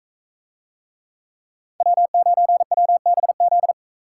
Morse code, heard as a keyed tone, about 40 wpm, 700 Hz. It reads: W9WBZ